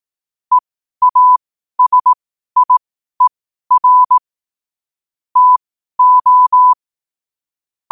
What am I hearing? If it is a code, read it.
Morse code, heard as a keyed tone, about 18 wpm, spaced as for 12 wpm, 1000 Hz. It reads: EASIER TO
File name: qotdfiles / 12wpm